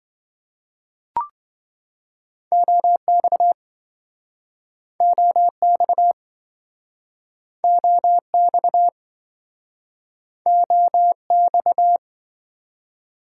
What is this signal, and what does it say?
Morse code, heard as a keyed tone, about 28 wpm, 700 Hz. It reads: OX OX OX OX